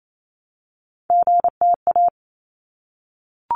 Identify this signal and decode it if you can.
Morse code, heard as a keyed tone, about 28 wpm, 700 Hz. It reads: GTA
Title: ARRL Sections 28wpm